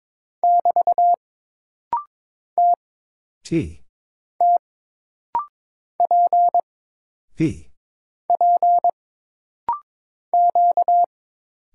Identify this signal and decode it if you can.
Morse code, heard as a keyed tone, about 22 wpm, 700 Hz. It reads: <BT> T T P P Q
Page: This rapid-fire timing is geared to push your ICR (Instant Character Recognition) skills.